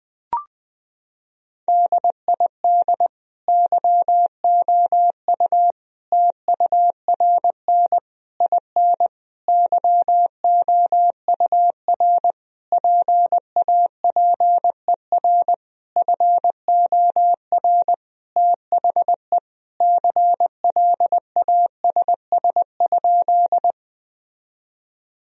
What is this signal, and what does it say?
Morse code, heard as a keyed tone, about 21 wpm, 700 Hz. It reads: DID YOU TURN IN YOUR PAPER FOR THE CLASS?
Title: Sentences from Top 1000 Words - Review Part 2 - 20wpm